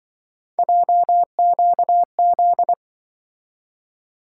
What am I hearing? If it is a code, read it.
Morse code, heard as a keyed tone, about 26 wpm, 700 Hz. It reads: JQZ